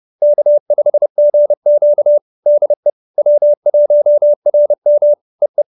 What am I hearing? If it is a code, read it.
Morse code, heard as a keyed tone, about 30 wpm, 590 Hz. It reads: K5GQ DE W1RM EE
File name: QSO308 30